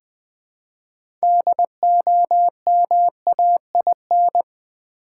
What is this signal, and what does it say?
Morse code, heard as a keyed tone, about 20 wpm, 700 Hz. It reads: DOMAIN
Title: 6 Letter Words - First 3 Letters Given - 20wpm